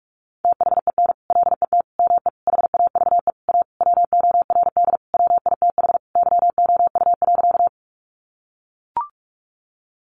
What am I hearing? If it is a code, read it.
Morse code, heard as a keyed tone, about 44 wpm, 700 Hz. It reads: THEN LET ME HAVE A WORD WITH YOU.
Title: Sentences from Top 300 Words 45wpm